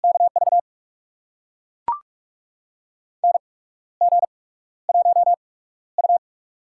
Morse code, heard as a keyed tone, about 44 wpm, spaced as for 15 wpm, 700 Hz. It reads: KV NG1U
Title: USA Call Signs - 4 Characters - 8x Character Spacing - 45wpm